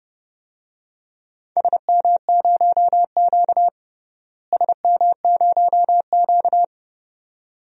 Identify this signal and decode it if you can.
Morse code, heard as a keyed tone, about 31 wpm, 700 Hz. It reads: SM0Q SM0Q
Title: DXCC-100 Call Signs - 4 Characters - Repeated 2x - 30wpm